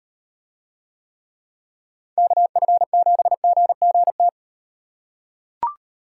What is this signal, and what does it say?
Morse code, heard as a keyed tone, about 38 wpm, 700 Hz. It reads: KF7GGT